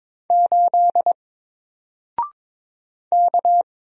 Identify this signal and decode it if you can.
Morse code, heard as a keyed tone, about 22 wpm, 700 Hz. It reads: 8 K
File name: Single-Letter-Number-22wpm